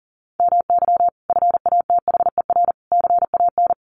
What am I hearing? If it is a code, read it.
Morse code, heard as a keyed tone, about 40 wpm, 700 Hz. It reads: MY FATHER CAN
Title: Sentences from Top 1500 Words 40wpm